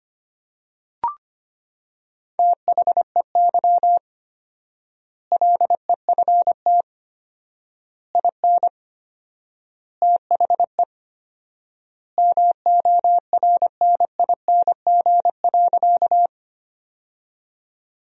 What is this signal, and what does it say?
Morse code, heard as a keyed tone, about 25 wpm, 700 Hz. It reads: THEY LEFT IN THE MORNING.
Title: 4x Word-Spacing - Sentences from Top 400 Words - 25wpm